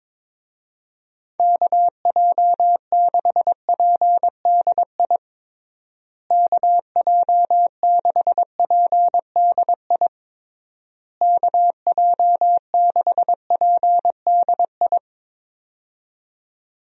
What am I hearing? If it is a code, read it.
Morse code, heard as a keyed tone, about 21 wpm, 700 Hz. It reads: KJ6PDI KJ6PDI KJ6PDI
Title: USA Call Signs - 6 Characters - Repeated 3x - 22wpm